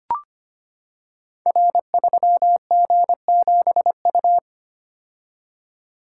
Morse code, heard as a keyed tone, about 25 wpm, 690 Hz. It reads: R3G7U
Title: Code Groups - 5 Letter:Numbers - 25wpm